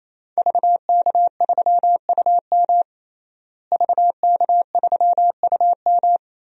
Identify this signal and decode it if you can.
Morse code, heard as a keyed tone, about 28 wpm, 700 Hz. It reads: VK3UM VK3UM
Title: DXCC-100 Call Signs - 5 Characters - Repeated 3x - 28wpm